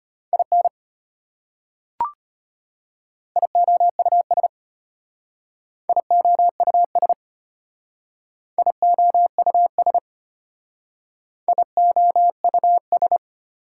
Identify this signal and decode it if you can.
Morse code, heard as a keyed tone, about 32 wpm, 700 Hz. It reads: IN IOUS IOUS IOUS IOUS